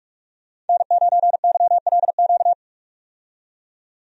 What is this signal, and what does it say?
Morse code, heard as a keyed tone, about 46 wpm, 700 Hz. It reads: N9YLQ
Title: USA Call Signs - 5 Characters - 45wpm